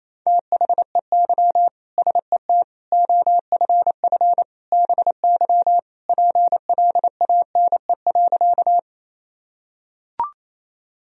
Morse code, heard as a keyed tone, about 28 wpm, 700 Hz. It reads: THEY SET OFF BY PLANE.
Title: Sentences from Top 1000 Words - Review Part 3 - 28wpm